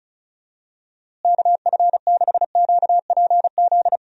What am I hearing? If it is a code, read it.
Morse code, heard as a keyed tone, about 34 wpm, 700 Hz. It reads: KF6QPZ
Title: USA Call Signs - 6 Characters - Repeated 3x - 35wpm